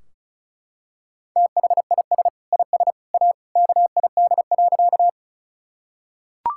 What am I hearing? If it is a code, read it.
Morse code, heard as a keyed tone, about 36 wpm, 700 Hz. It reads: THIS IS A KID.
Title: Sentences from Top 1600 Words 35wpm